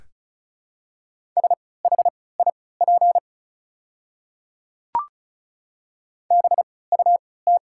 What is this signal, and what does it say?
Morse code, heard as a keyed tone, about 34 wpm, spaced as for 20 wpm, 690 Hz. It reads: SHIP BUT
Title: 3x Character Spacing - Top 500 Words - 35wpm